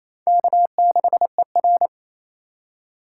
Morse code, heard as a keyed tone, about 37 wpm, 700 Hz. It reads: K6ER